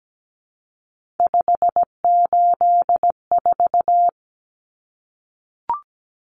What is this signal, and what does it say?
Morse code, heard as a keyed tone, about 17 wpm, 700 Hz. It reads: 584
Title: Numbers - 3-Digit 17wpm